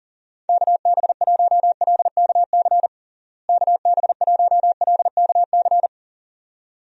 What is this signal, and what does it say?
Morse code, heard as a keyed tone, about 40 wpm, 700 Hz. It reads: KB1LKC KB1LKC
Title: USA Call Signs - 6 Characters - Repeated 2x - 40wpm